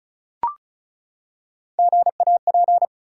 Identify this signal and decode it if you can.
Morse code, heard as a keyed tone, about 36 wpm, 700 Hz. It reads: GAP
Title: Three Letter Words - 35wpm